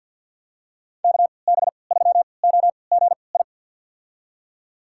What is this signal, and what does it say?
Morse code, heard as a keyed tone, about 47 wpm, spaced as for 28 wpm, 700 Hz. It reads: KB3OGI
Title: USA Call Signs - 6 Characters - 3x Character Spacing - 50wpm